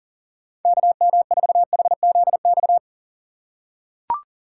Morse code, heard as a keyed tone, about 40 wpm, 700 Hz. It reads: KM4HZX
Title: USA Call Signs - 6 Characters - 3x Character Spacing - 40wpm